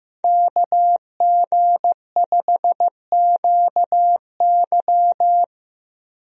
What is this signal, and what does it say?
Morse code, heard as a keyed tone, about 15 wpm, 700 Hz. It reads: KG5QY
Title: USA Call Signs - 5 Characters - 15wpm